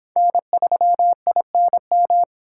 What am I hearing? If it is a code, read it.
Morse code, heard as a keyed tone, about 25 wpm, 700 Hz. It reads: N3INM